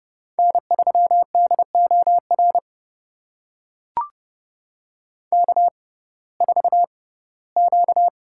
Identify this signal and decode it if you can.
Morse code, heard as a keyed tone, about 30 wpm, spaced as for 18 wpm, 700 Hz. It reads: N3DOR K4Q
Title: USA Call Signs - 5 Characters - 6x Character Spacing - 30wpm